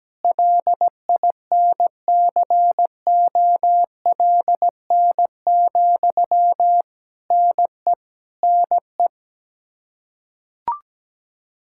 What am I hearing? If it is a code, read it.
Morse code, heard as a keyed tone, about 17 wpm, 700 Hz. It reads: LINCOLN, NE NE
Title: Practice QSO - Exchange 1 - Beginning - 17wpm